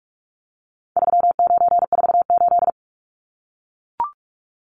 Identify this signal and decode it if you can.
Morse code, heard as a keyed tone, about 44 wpm, 700 Hz. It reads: 3948
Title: Numbers - 4-Digit 45wpm